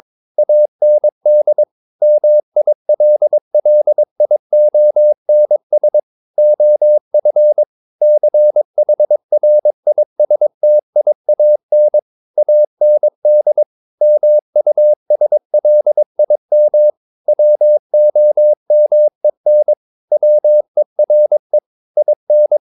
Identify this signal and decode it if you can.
Morse code, heard as a keyed tone, about 22 wpm, 600 Hz. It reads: AND MILLIONS OF CHRISTIAN AND MUSLIM WOMEN WERE IN
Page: Romanen Never af Ken Follet på engelsk.